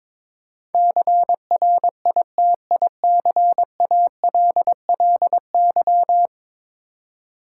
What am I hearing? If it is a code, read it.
Morse code, heard as a keyed tone, about 22 wpm, 700 Hz. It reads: CRITICALLY
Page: morse-code-ninja-practice-22wpm